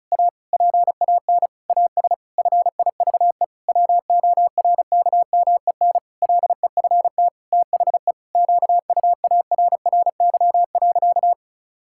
Morse code, heard as a keyed tone, about 36 wpm, 700 Hz. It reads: A PAN AS FIVE WORKMEN LEFT THE QUARRY.
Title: Pangram Challenge v2 - 35wpm